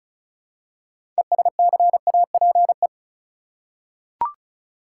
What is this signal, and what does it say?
Morse code, heard as a keyed tone, about 36 wpm, 700 Hz. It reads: ESCAPE
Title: Six Letter Words - 35wpm